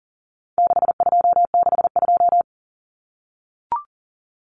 Morse code, heard as a keyed tone, about 40 wpm, 700 Hz. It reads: 6262
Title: Numbers - 4-Digit 40wpm